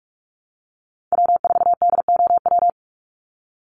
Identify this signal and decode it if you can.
Morse code, heard as a keyed tone, about 44 wpm, 700 Hz. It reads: W4DOW
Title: NA SOTA Chaser Callsigns 45wpm